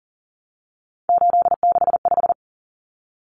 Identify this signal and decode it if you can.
Morse code, heard as a keyed tone, about 40 wpm, 700 Hz. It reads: 865
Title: Common Cut Numbers - 3-Digit 40wpm